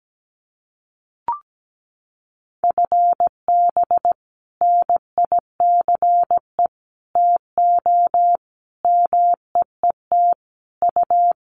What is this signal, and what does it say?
Morse code, heard as a keyed tone, about 17 wpm, 700 Hz. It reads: FB NICE TO MEET U
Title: Common QSO Phrases 17wpm